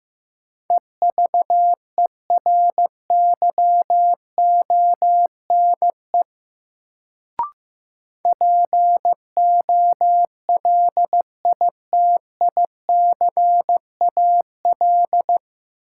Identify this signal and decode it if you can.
Morse code, heard as a keyed tone, about 15 wpm, 700 Hz. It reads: EVERYONE POLITICAL
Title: Sentences from Top 800 Words - New Word Drill - 15wpm